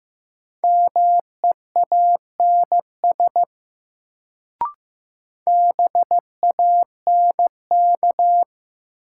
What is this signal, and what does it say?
Morse code, heard as a keyed tone, about 15 wpm, 700 Hz. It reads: MEANS BANK
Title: Sets of 5 Words - New Word Drill - 15wpm